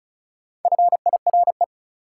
Morse code, heard as a keyed tone, about 36 wpm, 700 Hz. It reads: FIRE
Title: Sentences from Top 400 Words - New Word Drill - 35wpm